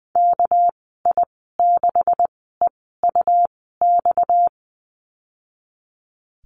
Morse code, heard as a keyed tone, about 20 wpm, spaced as for 14 wpm, 700 Hz. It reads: KI6EUX
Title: 231 Lesson 37g - Letter x - Call Signs 20wpm